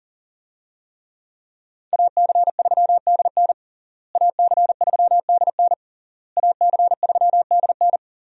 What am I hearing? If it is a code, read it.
Morse code, heard as a keyed tone, about 40 wpm, 700 Hz. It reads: AC3DN AC3DN AC3DN
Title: USA Call Signs - 5 Characters - Repeated 3x - 40wpm